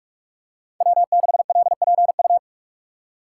Morse code, heard as a keyed tone, about 46 wpm, 700 Hz. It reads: W6LPU
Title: USA Call Signs - 2x Character Spacing - 45wpm